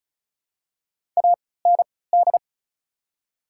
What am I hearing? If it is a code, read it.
Morse code, heard as a keyed tone, about 36 wpm, spaced as for 20 wpm, 700 Hz. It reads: AND
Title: 3x Character Spacing - Top 100 Words - 35wpm